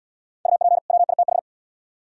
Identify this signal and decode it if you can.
Morse code, heard as a keyed tone, about 75 wpm, 700 Hz. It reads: UP WITH